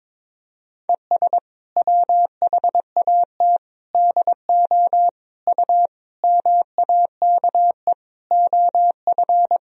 Morse code, heard as a keyed tone, about 22 wpm, 700 Hz. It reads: ES WHAT DO U MAKE OF
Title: Sentences from Top 100 Words and Abbreviations 22wpm